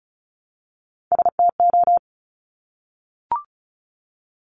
Morse code, heard as a keyed tone, about 36 wpm, 700 Hz. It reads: STO
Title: Top Three Letter Combinations in Words 35wpm